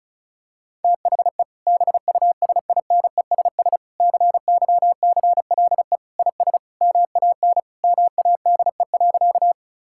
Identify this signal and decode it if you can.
Morse code, heard as a keyed tone, about 36 wpm, 690 Hz. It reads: THE BUSINESS CYCLE IS MAN MADE.
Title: Sentences from Top 1000 Words - Review Part 6 - 35wpm